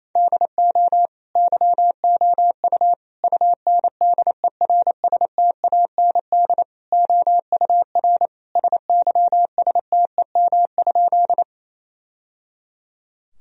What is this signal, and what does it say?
Morse code, heard as a keyed tone, about 27 wpm, 700 Hz. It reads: DO YOU UNDERSTAND OUR SYSTEM?
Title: Sentences from Top 1000 Words - Review Part 3 - 28wpm